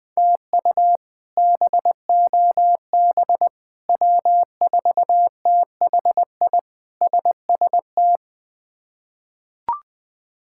Morse code, heard as a keyed tone, about 20 wpm, 700 Hz. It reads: TU BOB W4THI SST
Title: K1USN SST QSO Practice - 20wpm